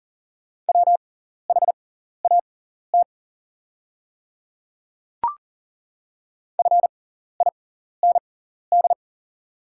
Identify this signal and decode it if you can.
Morse code, heard as a keyed tone, about 40 wpm, spaced as for 14 wpm, 690 Hz. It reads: WHAT FIND
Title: 6x Character Spacing - Top 100 Words - 40wpm